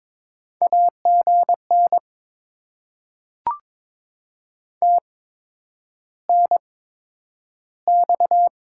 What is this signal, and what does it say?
Morse code, heard as a keyed tone, about 22 wpm, spaced as for 9 wpm, 700 Hz. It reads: AGN TNX
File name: Top 20 QSO Elements - 8x Character Spacing - 22wpm